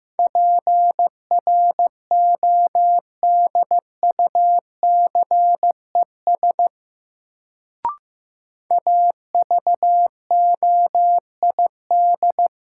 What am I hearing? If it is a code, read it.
Morse code, heard as a keyed tone, about 15 wpm, 700 Hz. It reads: PRODUCES AVOID
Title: Sentences from Top 1000 Words - Review Part 6 - New Word Drill - 15wpm